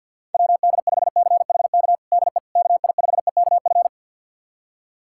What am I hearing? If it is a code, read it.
Morse code, heard as a keyed tone, about 51 wpm, 700 Hz. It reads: WD5CHK DE KI5EKF
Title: USA Callsign DE Callsign 50wpm